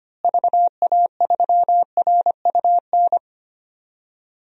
Morse code, heard as a keyed tone, about 25 wpm, 700 Hz. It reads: VA3RUN
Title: DXCC-100 Call Signs - 6 Characters - Repeated 3x - 25wpm